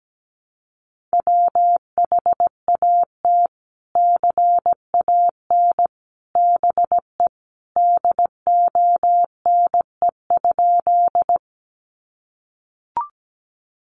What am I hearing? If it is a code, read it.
Morse code, heard as a keyed tone, about 17 wpm, 700 Hz. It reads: WHAT CAN BE DONE?
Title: Sentences from Top 500 Words 17wpm